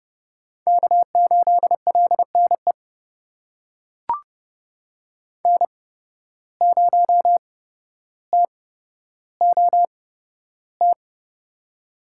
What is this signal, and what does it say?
Morse code, heard as a keyed tone, about 30 wpm, spaced as for 13 wpm, 700 Hz. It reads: K8LNE N0TOT
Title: USA Call Signs - 5 Characters - 8x Character Spacing - 30wpm